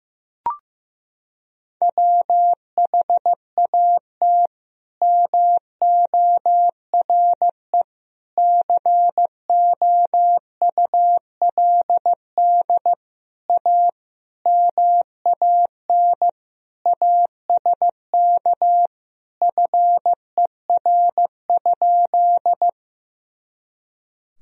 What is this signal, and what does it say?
Morse code, heard as a keyed tone, about 15 wpm, 700 Hz. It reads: WHAT MORE COULD A MAN ASK FER?
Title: Sentences from Top 200 Words and Abbreviations 15wpm